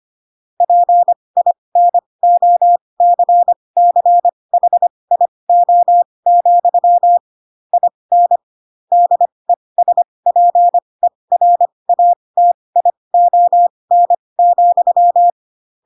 Morse code, heard as a keyed tone, about 25 wpm, spaced as for 20 wpm, 700 Hz. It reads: PINOCCHIO, IN DESPERATION,
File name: Pinocchio / Pinocchio ch6 15 25 ik1hge